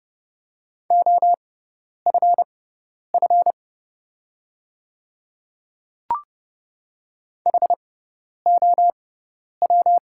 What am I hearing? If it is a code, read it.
Morse code, heard as a keyed tone, about 30 wpm, spaced as for 10 wpm, 700 Hz. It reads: OFF HOW
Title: 6x Character Spacing - Top 300 Words - 30wpm